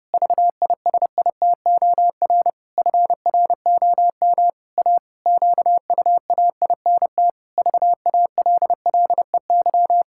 Morse code, heard as a keyed tone, about 30 wpm, 700 Hz. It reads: VISITOR FROM A QUAINT VALLEY
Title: Pangram Challenge v2 - 30wpm